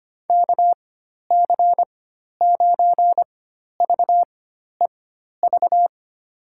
Morse code, heard as a keyed tone, about 25 wpm, spaced as for 12 wpm, 700 Hz. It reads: KC9VEV
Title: USA Call Signs - 6 Characters - 4x Character Spacing - 25wpm